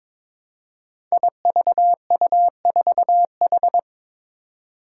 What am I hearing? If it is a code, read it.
Morse code, heard as a keyed tone, about 22 wpm, 700 Hz. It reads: IVU4H
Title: Five Challenging Letter-Number Combinations 22wpm